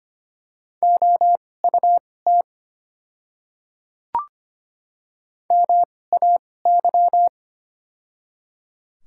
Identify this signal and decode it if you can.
Morse code, heard as a keyed tone, about 25 wpm, spaced as for 18 wpm, 700 Hz. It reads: OUT MAY
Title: 2x Character Spacing - Top 100 Words - 25wpm